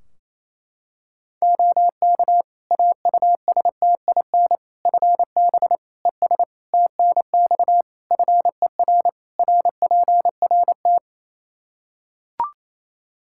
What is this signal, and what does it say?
Morse code, heard as a keyed tone, about 28 wpm, 700 Hz. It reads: OK AUSTIN FB ES TNX FER RPRT
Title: Practice QSO - Exchange 2 - Info - 28wpm